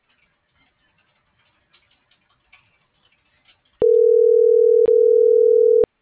Ringback tone (North America), 440 + 480 Hz.